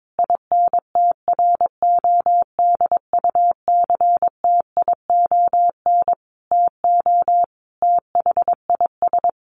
Morse code, heard as a keyed tone, about 22 wpm, 700 Hz. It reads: INTRODUCTION TO THIS
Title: Sentences from Top 1300 Words 22wpm